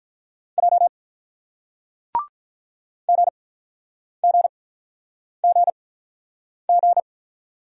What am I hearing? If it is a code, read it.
Morse code, heard as a keyed tone, about 46 wpm, spaced as for 9 wpm, 700 Hz. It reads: J GGGG